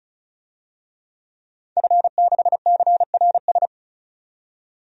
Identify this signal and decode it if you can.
Morse code, heard as a keyed tone, about 34 wpm, 700 Hz. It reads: F6CRS
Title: DXCC-100 Call Signs - 5 Characters - 35wpm